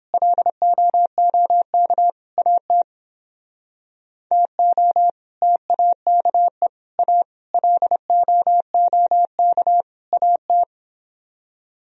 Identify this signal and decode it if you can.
Morse code, heard as a keyed tone, about 31 wpm, 700 Hz. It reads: LOOK AT TO TAKE A LOOK AT